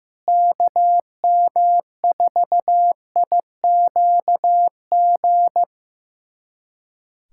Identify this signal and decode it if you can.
Morse code, heard as a keyed tone, about 16 wpm, 700 Hz. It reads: KM4IQG
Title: USA Callsign DE Callsign - 6 Characters - 15wpm